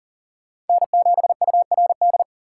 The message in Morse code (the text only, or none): N7URD